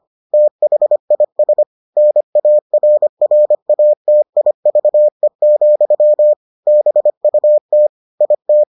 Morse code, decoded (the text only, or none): THIS NARRATIVE, BUT IT